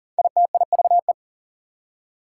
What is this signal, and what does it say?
Morse code, heard as a keyed tone, about 40 wpm, 700 Hz. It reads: ITIVE
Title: Prefixes and Suffixes 40wpm